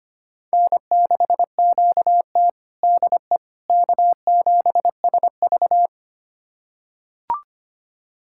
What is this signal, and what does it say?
Morse code, heard as a keyed tone, about 25 wpm, 700 Hz. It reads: N6QT DE K7SV
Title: USA Callsign DE Callsign - 4 Characters - 25wpm